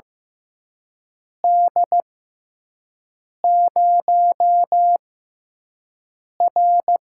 Morse code, heard as a keyed tone, about 15 wpm, spaced as for 5 wpm, 700 Hz. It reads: D0R
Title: USA Call Signs - 4 Characters - 6x Character Spacing - 15wpm